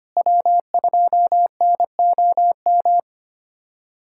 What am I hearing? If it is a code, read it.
Morse code, heard as a keyed tone, about 25 wpm, 700 Hz. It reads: W2NOM
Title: USA Callsign DE Callsign - 5 Characters - 25wpm